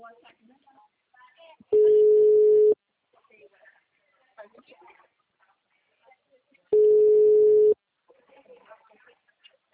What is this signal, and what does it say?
Ringback tone (Europe, 425 Hz), with speech.